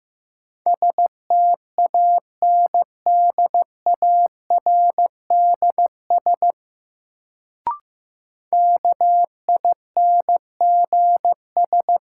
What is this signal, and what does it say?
Morse code, heard as a keyed tone, about 15 wpm, 700 Hz. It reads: STANDARDS KINGS
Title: Sentences from Top 1000 Words - Review Part 6 - New Word Drill - 15wpm